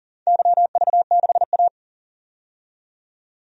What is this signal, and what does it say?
Morse code, heard as a keyed tone, about 40 wpm, 700 Hz. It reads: YV6A
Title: DXCC-100 Call Signs - 4 Characters - 40wpm